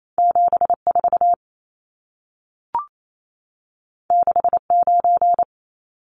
Morse code, heard as a keyed tone, about 28 wpm, 700 Hz. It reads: 74 69
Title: Numbers - 2-Digit 28wpm